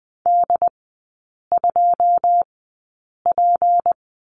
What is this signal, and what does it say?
Morse code, heard as a keyed tone, about 20 wpm, spaced as for 8 wpm, 690 Hz. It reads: D2P
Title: 159 Lesson 26e - Letter g - Three Characters 20wpm